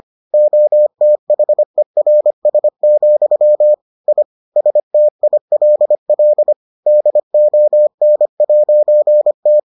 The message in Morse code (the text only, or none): OTHERS, I STILL DON'T